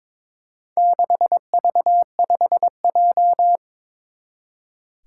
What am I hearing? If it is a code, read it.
Morse code, heard as a keyed tone, about 22 wpm, 700 Hz. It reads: 6V5J
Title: Four Challenging Letter-Number Combinations 22wpm